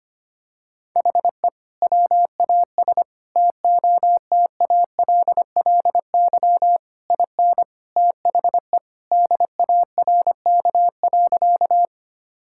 Morse code, heard as a keyed tone, about 25 wpm, 700 Hz. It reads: HE WAS TOTALLY IN THE DARK.
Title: Sentences from Top 1600 Words 25wpm